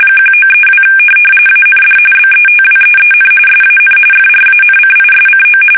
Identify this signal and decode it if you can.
RTTY, about 45.45 baud. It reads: 1932 AND BETWEEN SAN FRANCISCO
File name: fsk850 / teletype-RTTY